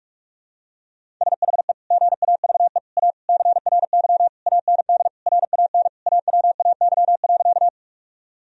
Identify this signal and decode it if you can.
Morse code, heard as a keyed tone, about 45 wpm, 700 Hz. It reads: SHE GAVE A CRY AND RAN AWAY.
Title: Sentences from Top 1000 Words - Review Part 3 - 45wpm